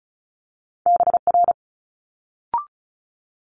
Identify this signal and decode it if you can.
Morse code, heard as a keyed tone, about 36 wpm, 700 Hz. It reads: BR
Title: Multi-letter Phonemes Quick Fire 35wpm